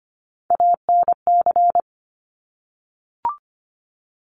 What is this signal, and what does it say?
Morse code, heard as a keyed tone, about 25 wpm, 700 Hz. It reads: ANC
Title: Top Three Letter Combinations in Words 25wpm